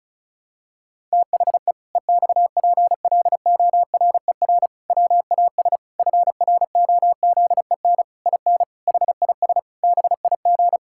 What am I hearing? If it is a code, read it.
Morse code, heard as a keyed tone, about 36 wpm, 700 Hz. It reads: THE EXPLORER WAS FROZEN IN HIS BIG
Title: Pangram Challenge v2 - 35wpm